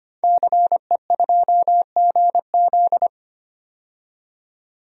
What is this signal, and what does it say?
Morse code, heard as a keyed tone, about 25 wpm, 700 Hz. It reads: CE2GZ
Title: DXCC-100 Call Signs - 5 Characters - 25wpm